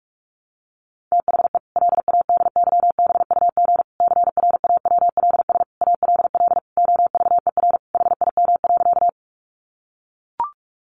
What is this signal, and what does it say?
Morse code, heard as a keyed tone, about 44 wpm, 700 Hz. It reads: THE LADYBUG CRAWLS ALL OVER HIM.
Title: Additional Easy Phrases 45wpm